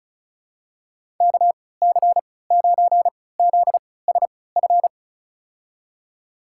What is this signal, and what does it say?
Morse code, heard as a keyed tone, about 34 wpm, spaced as for 20 wpm, 700 Hz. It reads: KC9ZSF
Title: USA Call Signs - 6 Characters - 3x Character Spacing - 35wpm